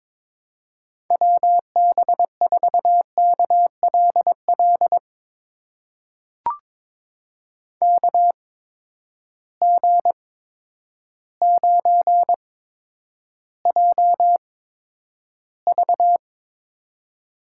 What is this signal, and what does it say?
Morse code, heard as a keyed tone, about 22 wpm, spaced as for 10 wpm, 700 Hz. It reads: WB4KLL KG9JV
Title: USA Call Signs - 8x Character Spacing - 22wpm